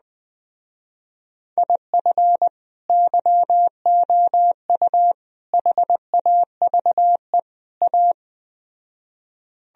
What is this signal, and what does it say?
Morse code, heard as a keyed tone, about 20 wpm, 700 Hz. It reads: IF YOU HAVE A
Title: Speed-Racing - Sets of 4 Words 17wpm